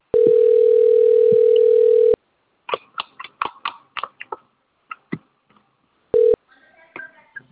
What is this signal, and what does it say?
Ringback tone (North America), 440 + 480 Hz.